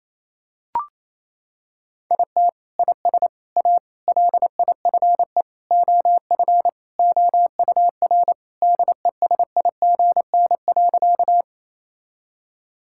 Morse code, heard as a keyed tone, about 28 wpm, 700 Hz. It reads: IT IS A LIFE OF OUR DESIGN.
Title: Sentences from Top 1000 Words - Review Part 5 - 28wpm